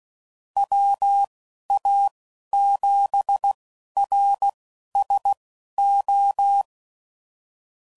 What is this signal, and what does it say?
Morse code, heard as a keyed tone, about 16 wpm, spaced as for 11 wpm, 800 Hz. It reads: WA7RSO
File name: CW Code Groups @15WPM / QSOs / WD0BGP Spaced